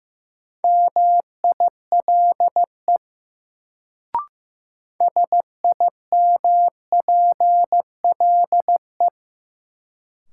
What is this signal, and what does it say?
Morse code, heard as a keyed tone, about 15 wpm, 700 Hz. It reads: MILE SIMPLE
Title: Sentences from Top 400 Words - New Word Drill - 15wpm